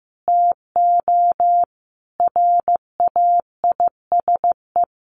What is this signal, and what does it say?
Morse code, heard as a keyed tone, about 15 wpm, 700 Hz. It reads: TO RAISE
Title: Sentences from Top 1400 Words 15wpm